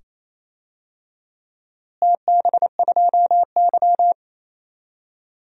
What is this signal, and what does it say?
Morse code, heard as a keyed tone, about 28 wpm, 700 Hz. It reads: TB2Y
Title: DXCC-100 Call Signs - 4 Characters - Repeated 2x - 28wpm